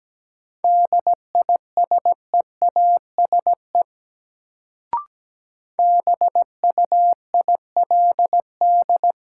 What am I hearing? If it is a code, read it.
Morse code, heard as a keyed tone, about 17 wpm, 700 Hz. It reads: DISEASE BUILD
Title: Sentences from Top 1100 Words - New Word Drill Alt - 17wpm